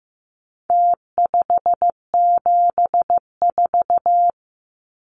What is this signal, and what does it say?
Morse code, heard as a keyed tone, about 15 wpm, 700 Hz. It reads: T574
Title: Numbers - Contest Style - 4-Digit 15wpm